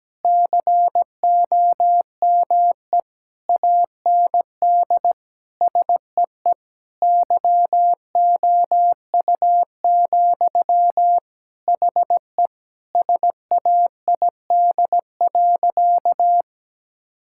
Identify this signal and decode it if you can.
Morse code, heard as a keyed tone, about 17 wpm, 700 Hz. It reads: COME AND SEE YOU, HE SAID.
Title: Sentences from Top 1000 Words - Review Part 6 - 17wpm